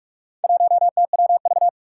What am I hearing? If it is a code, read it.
Morse code, heard as a keyed tone, about 44 wpm, 700 Hz. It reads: 1TWV
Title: Four Challenging Letter-Number Combinations 45wpm